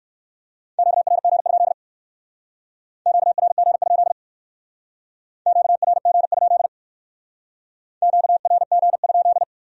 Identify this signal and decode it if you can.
Morse code, heard as a keyed tone, about 56 wpm, 700 Hz. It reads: QRG? QRG? QRG? QRG?
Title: Speed-Racing Q-Codes - 45wpm